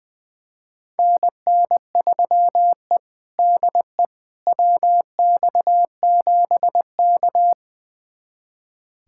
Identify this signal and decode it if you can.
Morse code, heard as a keyed tone, about 20 wpm, 700 Hz. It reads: NN3E DE WX7K
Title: USA Callsign DE Callsign - 4 Characters - 20wpm